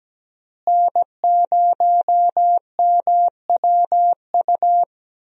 Morse code, heard as a keyed tone, about 17 wpm, 700 Hz. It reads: N0MWU
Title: USA Call Signs - 5 Characters - Repeated 3x - 17wpm